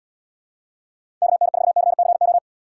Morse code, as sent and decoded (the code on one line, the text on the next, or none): -.- -. ....- -.-. -.-. ---
KN4CCO